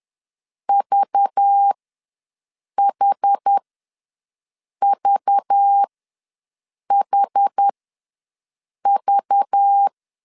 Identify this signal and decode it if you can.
Morse code, heard as a keyed tone, about 10 wpm, spaced as for 6 wpm, 800 Hz. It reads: VHVHV
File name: LEZIONI / 5 LBVC / 5 LBV ws 10wpm 3645